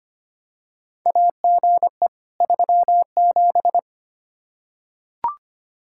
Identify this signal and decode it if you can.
Morse code, heard as a keyed tone, about 25 wpm, 700 Hz. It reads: AGE 37
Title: Practice QSO - Exchange 3 - More Info - 25wpm